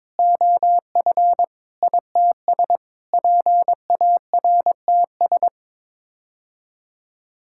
Morse code, heard as a keyed tone, about 22 wpm, 700 Hz. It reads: OF ITS PARTS
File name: practice / 22wpm / Sets of 6 Words - 22wpm